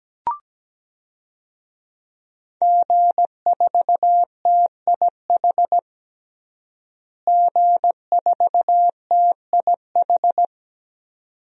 Morse code, heard as a keyed tone, about 17 wpm, 700 Hz. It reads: G4TIH G4TIH
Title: DXCC-100 Call Signs - 5 Characters - Repeated 2x - 17wpm